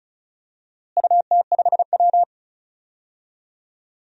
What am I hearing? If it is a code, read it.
Morse code, heard as a keyed tone, about 36 wpm, 700 Hz. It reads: UT5W
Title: DXCC-100 Call Signs - 4 Characters - Repeated 2x - 35wpm